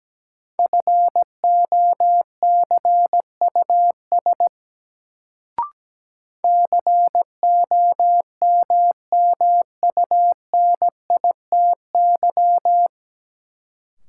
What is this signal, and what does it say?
Morse code, heard as a keyed tone, about 17 wpm, 700 Hz. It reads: FOCUS COMMUNITY
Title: Sentences from Top 1000 Words - Review Part 4 - New Word Drill - 17wpm